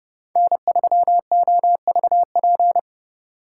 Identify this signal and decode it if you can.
Morse code, heard as a keyed tone, about 30 wpm, 690 Hz. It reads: N3OVP
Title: USA Call Signs - 5 Characters - 2x Character Spacing - 30wpm